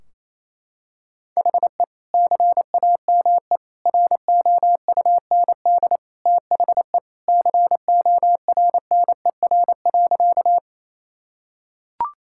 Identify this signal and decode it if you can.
Morse code, heard as a keyed tone, about 28 wpm, 700 Hz. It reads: HE CAME ROUND THE CORNER.
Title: Sentences from Top 1600 Words 28wpm